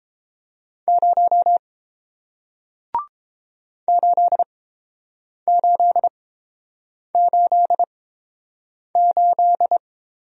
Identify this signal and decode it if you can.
Morse code, heard as a keyed tone, about 30 wpm, spaced as for 7 wpm, 700 Hz. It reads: 0 8888